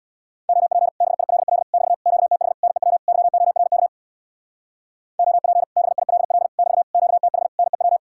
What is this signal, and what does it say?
Morse code, heard as a keyed tone, about 62 wpm, 700 Hz. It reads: OP BILL <BT> QTH NEW YORK OP BILL <BT> QTH NEW
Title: Speed-Racing Common QSO Phrases - 50wpm